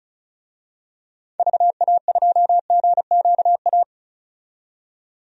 Morse code, heard as a keyed tone, about 34 wpm, 700 Hz. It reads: VA2GQA
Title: DXCC-100 Call Signs - 6 Characters - 35wpm